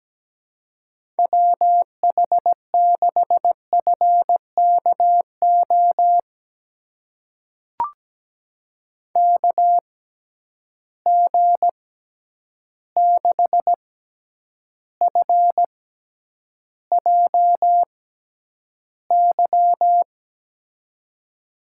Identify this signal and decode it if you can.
Morse code, heard as a keyed tone, about 17 wpm, spaced as for 9 wpm, 700 Hz. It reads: WH6FKO KG6FJY
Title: USA Call Signs - 6 Characters - 6x Character Spacing - 17wpm